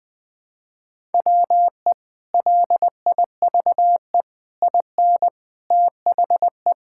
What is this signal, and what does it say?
Morse code, heard as a keyed tone, about 20 wpm, 700 Hz. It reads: WE LIVE IN THE
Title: Sentences with 3-Word Phrases - 20wpm